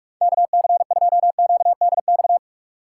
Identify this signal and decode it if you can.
Morse code, heard as a keyed tone, about 46 wpm, 700 Hz. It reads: K C 2 Q D X
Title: USA Call Signs - 6 Characters - 6x Character Spacing - 45wpm